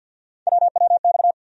Morse code, heard as a keyed tone, about 47 wpm, 690 Hz. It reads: WWX
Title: USA Call Sign Postfixes - 3 Letters 50wpm